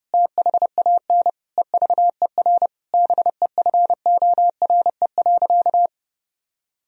Morse code, heard as a keyed tone, about 30 wpm, 700 Hz. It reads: THAN EVER BEFORE.
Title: Sentences from Top 1000 Words - Review Part 3 - 30wpm